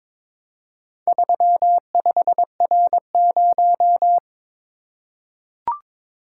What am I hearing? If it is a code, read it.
Morse code, heard as a keyed tone, about 22 wpm, 700 Hz. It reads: 35R0
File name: practice / 22wpm / Four Letter-Number Combinations - 22wpm